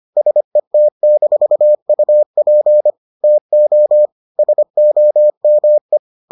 Morse code, heard as a keyed tone, about 25 wpm, 600 Hz. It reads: SET-UP TO SOME